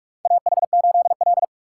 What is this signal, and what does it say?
Morse code, heard as a keyed tone, about 41 wpm, 700 Hz. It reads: AH8L
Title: USA Call Signs - 4 Characters - 6x Character Spacing - 45wpm